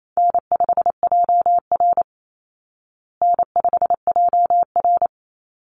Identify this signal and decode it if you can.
Morse code, heard as a keyed tone, about 28 wpm, 700 Hz. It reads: N5JR N5JR
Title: Call Signs Repeated 3x - 28wpm